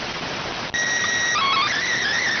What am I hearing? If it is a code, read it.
SSTV (slow-scan television), PD 290.